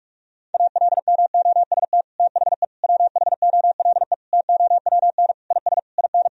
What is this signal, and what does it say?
Morse code, heard as a keyed tone, about 46 wpm, 700 Hz. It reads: ALMOST THE WHOLE TOWN IS IN
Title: Sentences from Top 1000 Words - Review Part 4 - 45wpm